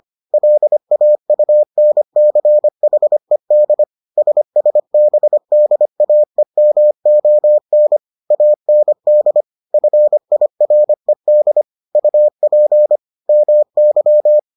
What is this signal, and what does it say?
Morse code, heard as a keyed tone, about 25 wpm, 600 Hz. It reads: LAUNCHED SSBDAEMON AND FIRED UP MY